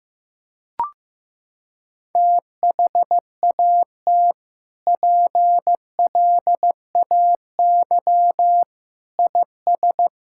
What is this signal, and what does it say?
Morse code, heard as a keyed tone, about 16 wpm, 700 Hz. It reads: THAT PLAY IS
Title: Sentences from Top 1000 Words - Review Part 5 - 15wpm